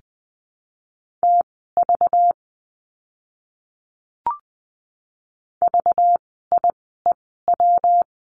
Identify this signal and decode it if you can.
Morse code, heard as a keyed tone, about 20 wpm, spaced as for 14 wpm, 700 Hz. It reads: TV VIEW
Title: 192 Lesson 31f - Letter v - Words 20wpm